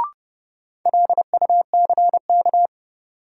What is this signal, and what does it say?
Morse code, heard as a keyed tone, about 32 wpm, 700 Hz. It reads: LUCK
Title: Sentences from Top 1500 Words - New Word Drill - 30wpm